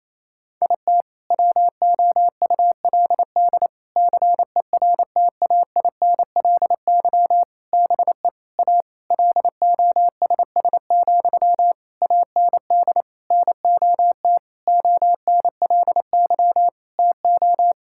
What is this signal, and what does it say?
Morse code, heard as a keyed tone, about 28 wpm, 700 Hz. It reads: IT WOULD CERTAINLY BE A LOSS, AND NOT ONLY TO
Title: Sentences from Top 1600 Words 28wpm